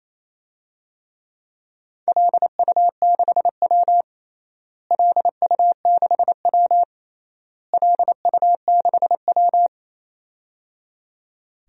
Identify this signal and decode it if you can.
Morse code, heard as a keyed tone, about 28 wpm, 700 Hz. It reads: LU6W LU6W LU6W
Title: DXCC-100 Call Signs - 4 Characters - Repeated 3x - 28wpm